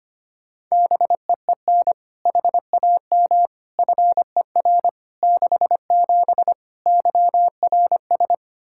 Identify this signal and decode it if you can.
Morse code, heard as a keyed tone, about 25 wpm, 700 Hz. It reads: BEEN HAM FER 67 YRS
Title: Practice QSO - Exchange 3 - More Info - 25wpm